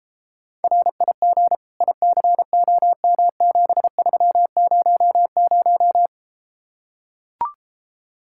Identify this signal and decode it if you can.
Morse code, heard as a keyed tone, about 33 wpm, 700 Hz. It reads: RIG ICOM7300